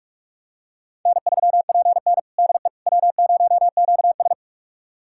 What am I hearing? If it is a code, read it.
Morse code, heard as a keyed tone, about 46 wpm, 700 Hz. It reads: N3PN DE W0QS
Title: USA Callsign DE Callsign - 4 Characters - 45wpm